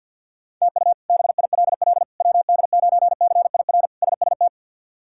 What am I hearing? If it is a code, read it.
Morse code, heard as a keyed tone, about 50 wpm, 700 Hz. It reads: TU BILL WD9CIR SST
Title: K1USN SST QSO Practice - 50wpm